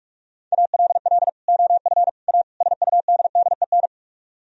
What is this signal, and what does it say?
Morse code, heard as a keyed tone, about 45 wpm, 700 Hz. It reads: ALL OF A SUDDEN
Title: Speed-Racing - Sets of 4 Words 45wpm